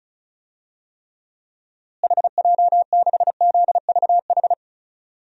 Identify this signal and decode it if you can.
Morse code, heard as a keyed tone, about 34 wpm, 700 Hz. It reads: HJ6ZVH
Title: DXCC-100 Call Signs - 6 Characters - Repeated 3x - 35wpm